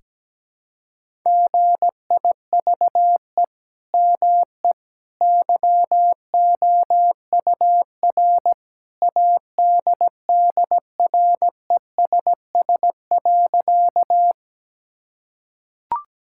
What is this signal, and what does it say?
Morse code, heard as a keyed tone, about 17 wpm, 700 Hz. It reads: GIVE ME YOUR ADDRESS.
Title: Sentences from Top 1000 Words - Review Part 6 - 17wpm